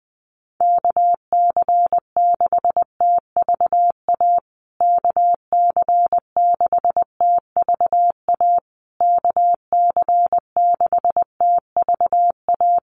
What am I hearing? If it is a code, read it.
Morse code, heard as a keyed tone, about 20 wpm, 700 Hz. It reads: KC6TVA KC6TVA KC6TVA
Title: USA Call Signs Repeated 3x 20wpm